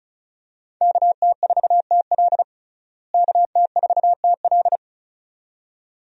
Morse code, heard as a keyed tone, about 34 wpm, 700 Hz. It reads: KT4TL KT4TL
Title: USA Call Signs - 5 Characters - Repeated 2x - 35wpm